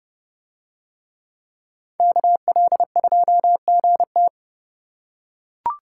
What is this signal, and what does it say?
Morse code, heard as a keyed tone, about 28 wpm, 700 Hz. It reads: KL2GT